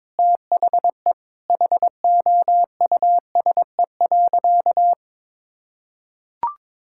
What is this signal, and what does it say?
Morse code, heard as a keyed tone, about 22 wpm, 700 Hz. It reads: THE HOUSE.
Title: Sentences from Top 1000 Words - Review Part 6 - 22wpm